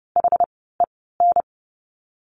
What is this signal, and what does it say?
Morse code, heard as a keyed tone, about 30 wpm, spaced as for 17 wpm, 700 Hz. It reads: HEN
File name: course / 30:10wpm / 237 Lesson 38f - Letter q - Words 30wpm